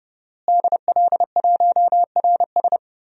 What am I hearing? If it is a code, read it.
Morse code, heard as a keyed tone, about 30 wpm, 700 Hz. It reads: DL1RS
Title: DXCC-100 Call Signs - 5 Characters - Repeated 3x - 30wpm